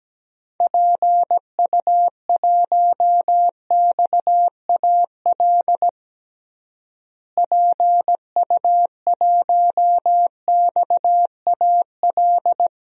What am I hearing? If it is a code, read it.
Morse code, heard as a keyed tone, about 17 wpm, 700 Hz. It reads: PU1XAL PU1XAL
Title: DXCC-100 Call Signs - 6 Characters - Repeated 3x - 17wpm